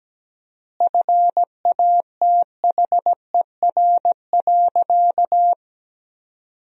Morse code, heard as a keyed tone, about 17 wpm, 700 Hz. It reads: FATHER.
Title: 4x Word-Spacing - Sentences from Top 500 Words - 17wpm